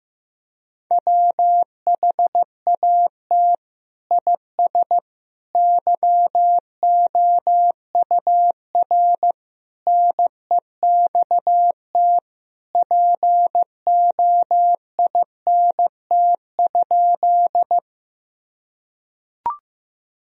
Morse code, heard as a keyed tone, about 15 wpm, 700 Hz. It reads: WHAT IS YOUR NEXT POINT?
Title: Sentences from Top 1000 Words - Review Part 1 - 15wpm